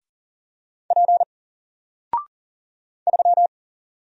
Morse code, heard as a keyed tone, about 38 wpm, 700 Hz. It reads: P 3
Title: Single Character (Letter:Number:Punctuation:Prosign) Alt 40wpm